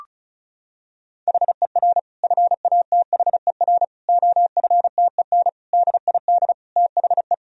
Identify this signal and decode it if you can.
Morse code, heard as a keyed tone, about 36 wpm, 700 Hz. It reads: HER FATHER OFTEN DID THE
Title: Sentences from Top 1000 Words - Review Part 1 - 35wpm